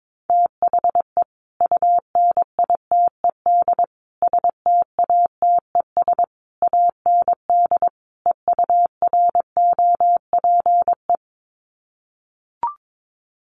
Morse code, heard as a keyed tone, about 22 wpm, 700 Hz. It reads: THE UNITED STATES AND EUROPE
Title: Sets of 5 Words Encore 22wpm